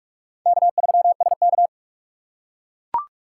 Morse code, heard as a keyed tone, about 45 wpm, 700 Hz. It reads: K3SK
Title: USA Call Signs - 4 Characters - 4x Character Spacing - 45wpm